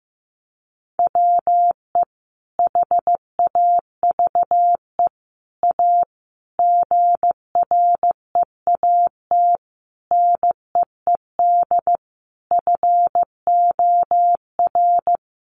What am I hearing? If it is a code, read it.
Morse code, heard as a keyed tone, about 15 wpm, 700 Hz. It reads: WE HAVE A GREAT NEED FOR
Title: Sentences from Top 1500 Words 15wpm